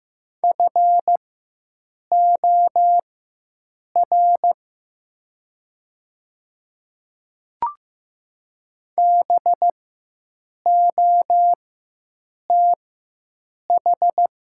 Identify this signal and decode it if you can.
Morse code, heard as a keyed tone, about 15 wpm, spaced as for 7 wpm, 700 Hz. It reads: FOR BOTH
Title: 4x Character Spacing - Top 300 Words - 15wpm